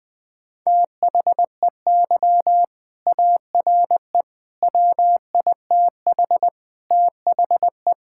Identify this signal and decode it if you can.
Morse code, heard as a keyed tone, about 20 wpm, 700 Hz. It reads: THEY ARE WITH THE
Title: Sentences from Top 1000 Words - Review Part 5 - 20wpm